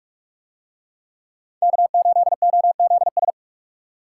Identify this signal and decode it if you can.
Morse code, heard as a keyed tone, about 44 wpm, 700 Hz. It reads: K8OZS